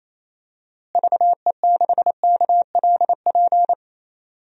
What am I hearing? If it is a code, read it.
Morse code, heard as a keyed tone, about 28 wpm, 700 Hz. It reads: VE6KLP
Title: DXCC-100 Call Signs - 6 Characters - Repeated 2x - 28wpm